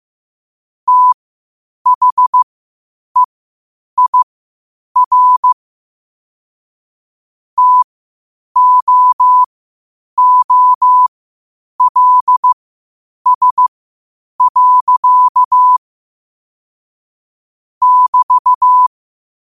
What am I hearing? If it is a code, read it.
Morse code, heard as a keyed tone, about 15 wpm, spaced as for 8 wpm, 1000 Hz. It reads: THEIR TOOLS. <BT>
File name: 8wpm